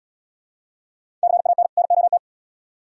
Morse code, heard as a keyed tone, about 66 wpm, 700 Hz. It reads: CAN NOT